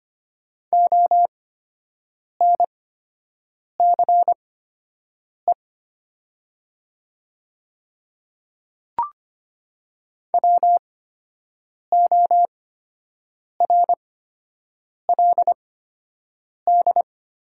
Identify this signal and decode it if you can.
Morse code, heard as a keyed tone, about 25 wpm, spaced as for 7 wpm, 700 Hz. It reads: ONCE WORLD
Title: 8x Character Spacing - Top 300 Words - 25wpm